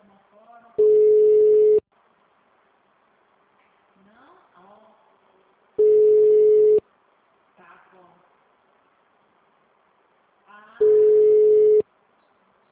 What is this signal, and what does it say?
Ringback tone (Europe, 425 Hz), with speech.